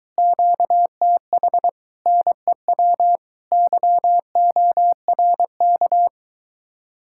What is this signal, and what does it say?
Morse code, heard as a keyed tone, about 23 wpm, 700 Hz. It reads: QTH NEW YORK